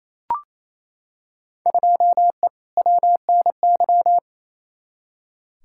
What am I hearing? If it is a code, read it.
Morse code, heard as a keyed tone, about 28 wpm, 700 Hz. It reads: 2E WNY
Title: Field Day Exchanges Encore - 28wpm